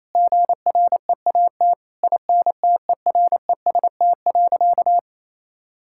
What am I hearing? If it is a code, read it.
Morse code, heard as a keyed tone, about 28 wpm, 700 Hz. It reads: GREAT INTEREST.
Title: Sentences from Top 1000 Words - Review Part 6 - 28wpm